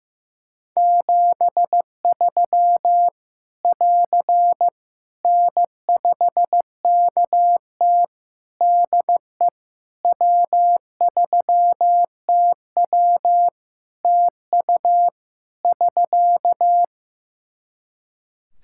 Morse code, heard as a keyed tone, about 15 wpm, 700 Hz. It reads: 73 <AR> N5KT DE W3TW TU <SK>
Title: Practice QSO - Exchange 4 - Ending - 15wpm